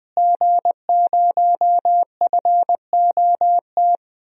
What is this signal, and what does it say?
Morse code, heard as a keyed tone, about 20 wpm, 700 Hz. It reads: G0FOT
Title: DXCC-100 Call Signs - 5 Characters - Repeated 3x - 20wpm